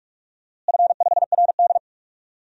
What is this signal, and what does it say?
Morse code, heard as a keyed tone, about 44 wpm, 700 Hz. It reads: F5RD
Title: DXCC-100 Call Signs - 4 Characters - Repeated 2x - 45wpm